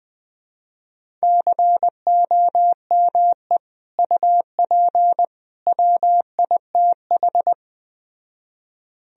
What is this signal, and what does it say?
Morse code, heard as a keyed tone, about 21 wpm, 700 Hz. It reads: COME UP WITH